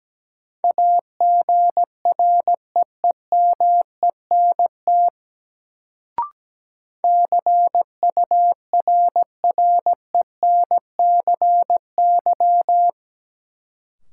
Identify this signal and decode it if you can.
Morse code, heard as a keyed tone, about 17 wpm, 700 Hz. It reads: AGREEMENT CURRENCY
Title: Sentences from Top 1500 Words - New Word Drill - 17wpm